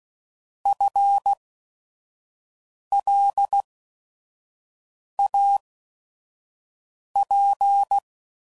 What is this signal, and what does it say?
Morse code, heard as a keyed tone, about 16 wpm, spaced as for 5 wpm, 800 Hz. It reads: FLAP
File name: Prefixes / By First Letter / Fs